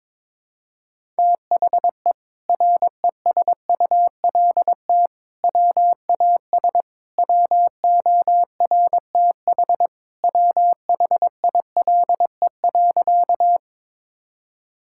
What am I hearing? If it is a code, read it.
Morse code, heard as a keyed tone, about 22 wpm, 700 Hz. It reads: THE RESULT WAS WORTH WHILE.
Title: Sentences from Top 1000 Words - Review Part 5 - 22wpm